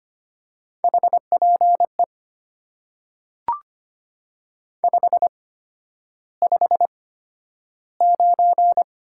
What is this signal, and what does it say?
Morse code, heard as a keyed tone, about 25 wpm, spaced as for 11 wpm, 700 Hz. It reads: HPE 559
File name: Top 90 QSO Elements - 8x Character Spacing - 25wpm